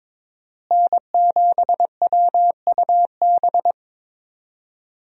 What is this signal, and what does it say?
Morse code, heard as a keyed tone, about 22 wpm, 700 Hz. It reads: N7WUB
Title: USA Call Signs - 5 Characters - Repeated 2x - 22wpm